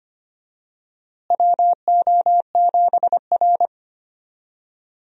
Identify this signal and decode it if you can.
Morse code, heard as a keyed tone, about 25 wpm, 700 Hz. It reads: WO7R
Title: USA Call Signs - 4 Characters - Repeated 3x - 25wpm